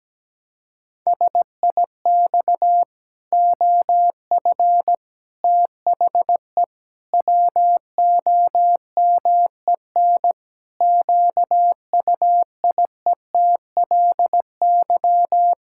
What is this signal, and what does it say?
Morse code, heard as a keyed tone, about 17 wpm, 700 Hz. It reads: SIX OF THE WOMEN QUIETLY
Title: Pangram Challenge v2 - 17wpm